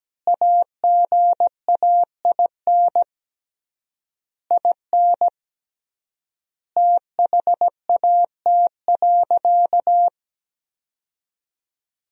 Morse code, heard as a keyed tone, about 17 wpm, 700 Hz. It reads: AGAIN IN THAT.
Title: 3x Word-Spacing - Sentences from Top 200 Words - 17wpm